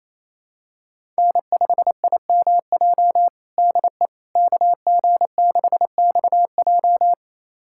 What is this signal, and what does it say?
Morse code, heard as a keyed tone, about 27 wpm, 700 Hz. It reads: N5IMJ DE KG6XJ
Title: USA Callsign DE Callsign - 5 Characters - 28wpm